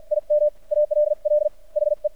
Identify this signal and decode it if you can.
Morse code, heard as a keyed tone, about 52 wpm, spaced as for 43 wpm, 600 Hz. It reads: IM ALL HE
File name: archived / CRSnet